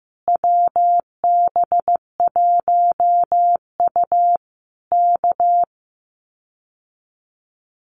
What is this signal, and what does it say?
Morse code, heard as a keyed tone, about 15 wpm, 700 Hz. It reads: WB1U K
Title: Common QSO Phrases Encore 15wpm